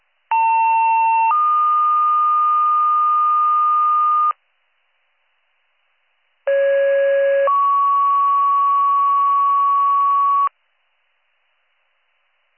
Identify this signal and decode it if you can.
Two-tone page, heard as a one-second tone, then a three-second tone, about 900 then 1220 Hz, about 550 then 1120 Hz. It is heard from 0.3 to 10.5 s.